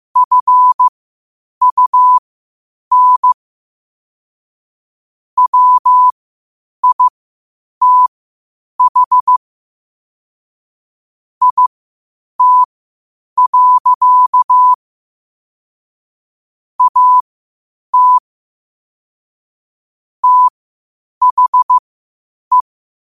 Morse code, heard as a keyed tone, about 15 wpm, spaced as for 8 wpm, 1000 Hz. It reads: FUN WITH IT. AT THE
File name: qotdfiles / 8wpm